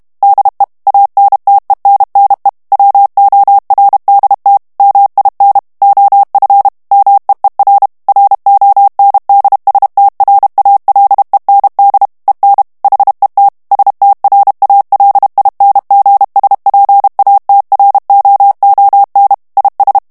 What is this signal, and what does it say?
Morse code, heard as a keyed tone, about 32 wpm, 800 Hz. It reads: DE ANTENNE WORDT MIN OF MEER RONDSTRALEND EN HET STRALINGSPATROON IS
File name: morse mp3 / HOLLANDAIS texte clair 1 vitesse 30 mots minute